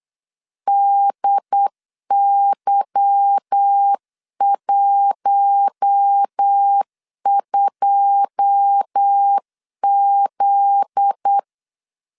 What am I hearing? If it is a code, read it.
Morse code, heard as a keyed tone, about 8 wpm, 800 Hz. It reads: DY12Z
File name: LEZIONI / 9 28 8wpm 3512